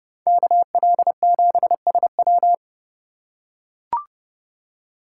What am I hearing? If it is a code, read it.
Morse code, heard as a keyed tone, about 29 wpm, 700 Hz. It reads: KL7SW